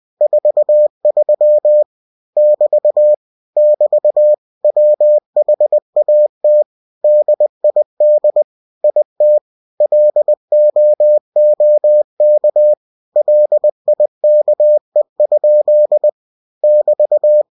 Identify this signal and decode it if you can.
Morse code, heard as a keyed tone, about 20 wpm, 600 Hz. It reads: 43 <BT> <BT> WHAT DID IT LOOK LIKE? <BT>
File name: cw-books / winnie-the-pooh / winnie-20wpm-page-0043